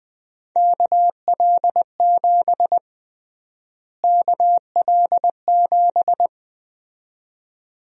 Morse code, heard as a keyed tone, about 20 wpm, 700 Hz. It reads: KL7 KL7
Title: USA Call Sign Prefixes - Repeated 2x - 20wpm